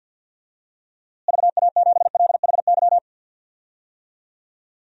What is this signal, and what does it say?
Morse code, heard as a keyed tone, about 49 wpm, 700 Hz. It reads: VA7LHY
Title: DXCC-100 Call Signs - 6 Characters - Repeated 3x - 50wpm